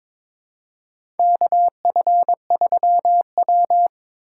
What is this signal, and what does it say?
Morse code, heard as a keyed tone, about 22 wpm, 700 Hz. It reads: KF3W
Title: USA Call Signs - 4 Characters - 8x Character Spacing - 22wpm